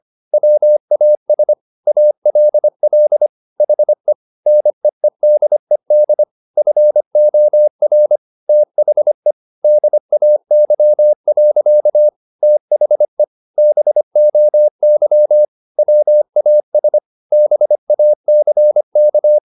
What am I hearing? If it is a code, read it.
Morse code, heard as a keyed tone, about 25 wpm, 600 Hz. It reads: WAS ALL HE NEEDED FOR THE DAY. THE BOY WAS BACK
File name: the old man and the sea part 1 25wpm